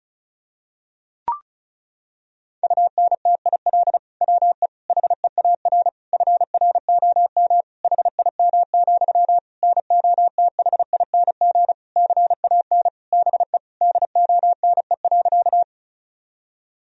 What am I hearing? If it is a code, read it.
Morse code, heard as a keyed tone, about 36 wpm, 700 Hz. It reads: UNTIL WE HEAR FROM HIM, NOTHING CAN BE DONE.
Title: Sentences from Top 1000 Words - Review Part 1 - 35wpm